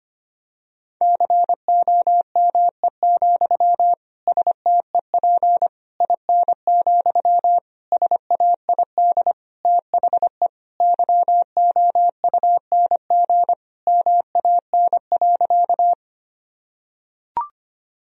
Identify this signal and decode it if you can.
Morse code, heard as a keyed tone, about 25 wpm, 700 Hz. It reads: COME, STEP IN, SAID THE YOUNG MAN.
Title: Sentences from Top 1000 Words - Review Part 2 - 25wpm